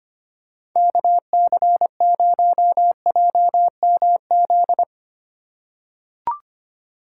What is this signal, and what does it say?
Morse code, heard as a keyed tone, about 25 wpm, 700 Hz. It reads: KC0JMZ
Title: USA Call Signs - 6 Characters - 3x Character Spacing - 25wpm